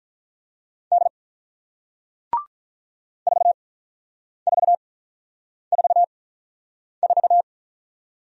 Morse code, heard as a keyed tone, about 48 wpm, spaced as for 9 wpm, 700 Hz. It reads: D 4444